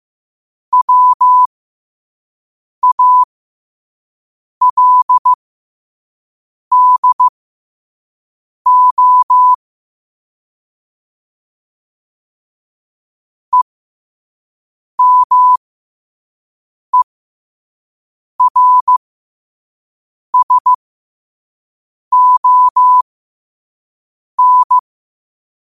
Morse code, heard as a keyed tone, about 15 wpm, spaced as for 5 wpm, 1000 Hz. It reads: WALDO EMERSON
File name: qotdfiles / 5wpm